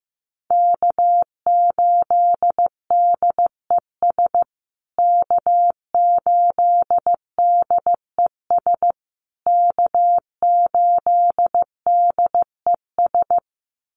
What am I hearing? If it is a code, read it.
Morse code, heard as a keyed tone, about 15 wpm, 700 Hz. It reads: K8DES K8DES K8DES
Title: USA Call Signs Repeated 3x 15wpm